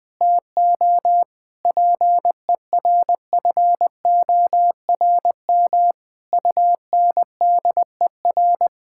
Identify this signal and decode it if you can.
Morse code, heard as a keyed tone, about 20 wpm, 700 Hz. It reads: TO PERFORM UNDER
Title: Sentences from Top 1000 Words - Review Part 6 - 20wpm